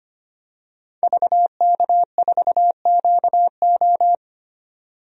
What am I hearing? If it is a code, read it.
Morse code, heard as a keyed tone, about 25 wpm, 700 Hz. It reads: VK4QO
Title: DXCC-100 Call Signs - 5 Characters - 25wpm